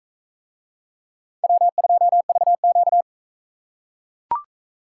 Morse code, heard as a keyed tone, about 40 wpm, 700 Hz. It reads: W2VQ